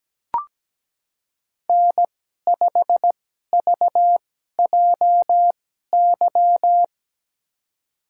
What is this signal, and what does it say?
Morse code, heard as a keyed tone, about 17 wpm, spaced as for 12 wpm, 700 Hz. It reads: N5VJY
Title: USA Call Signs - 5 Characters - 2x Character Spacing - 17wpm